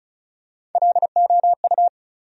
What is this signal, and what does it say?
Morse code, heard as a keyed tone, about 35 wpm, 700 Hz. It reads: LOU
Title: USA Common Names - Max 4 Letters - 35wpm